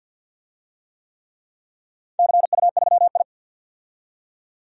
Morse code, heard as a keyed tone, about 46 wpm, 700 Hz. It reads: CU3I
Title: DXCC-100 Call Signs - 4 Characters - 50wpm